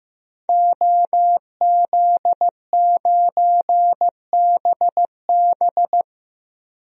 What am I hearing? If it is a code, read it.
Morse code, heard as a keyed tone, about 15 wpm, 700 Hz. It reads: OZ9BB
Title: DXCC-100 Call Signs - 5 Characters - 15wpm